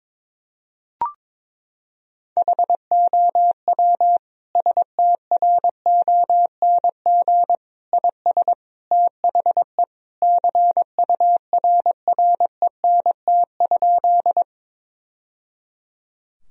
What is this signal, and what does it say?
Morse code, heard as a keyed tone, about 22 wpm, 700 Hz. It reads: HOW STRONG IS THE CURRENT?
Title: Sentences from Top 1000 Words - Review Part 4 - 22wpm